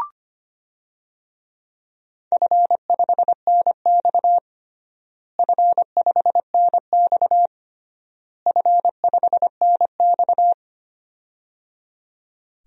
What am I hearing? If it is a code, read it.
Morse code, heard as a keyed tone, about 25 wpm, 700 Hz. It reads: F5NX F5NX F5NX
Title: DXCC-100 Call Signs - 4 Characters - Repeated 3x - 25wpm